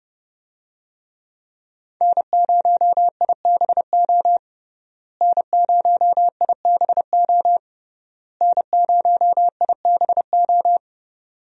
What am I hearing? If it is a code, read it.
Morse code, heard as a keyed tone, about 32 wpm, 700 Hz. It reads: N0IBO N0IBO N0IBO
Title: USA Call Signs - 5 Characters - Repeated 3x - 30wpm